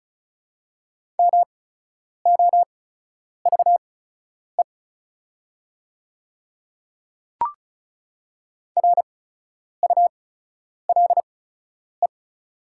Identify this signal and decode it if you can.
Morse code, heard as a keyed tone, about 36 wpm, spaced as for 10 wpm, 700 Hz. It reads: MOVE RULE
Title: 8x Character Spacing - Top 500 Words - 35wpm